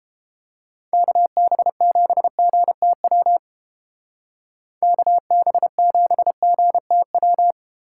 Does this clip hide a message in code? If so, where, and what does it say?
Yes: Morse code from 0.9 to 7.5 s, heard as a keyed tone, about 31 wpm, 700 Hz. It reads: KB7GTW KB7GTW